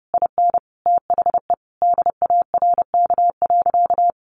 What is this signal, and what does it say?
Morse code, heard as a keyed tone, about 30 wpm, 700 Hz. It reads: IN THE DARK.
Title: Sentences from Top 1100 Words 30wpm